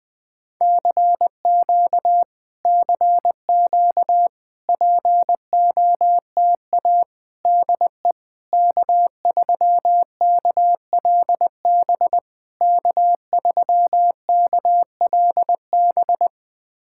Morse code, heard as a keyed tone, about 20 wpm, 700 Hz. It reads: CQ CQ POTA DE K3KLB K3KLB
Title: POTA QSO Practice - 20wpm